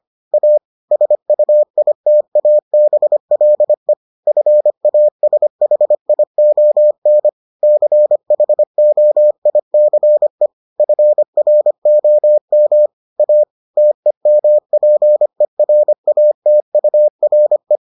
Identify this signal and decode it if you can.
Morse code, heard as a keyed tone, about 25 wpm, 600 Hz. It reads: A SUITABLE FASHION CHOICE FROM A TEMPERATURE